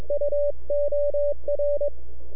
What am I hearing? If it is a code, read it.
Morse code, heard as a keyed tone, about 21 wpm, 580 Hz, with radio noise. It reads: UOR
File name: NAQPCW-8-2009 / W1UJ NAQP CW 09 2300Z